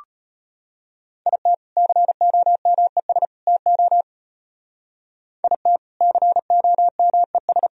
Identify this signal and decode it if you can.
Morse code, heard as a keyed tone, about 31 wpm, 700 Hz. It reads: IT COMES TO IT COMES
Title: Speed-Racing - Sets of 3 Words 25wpm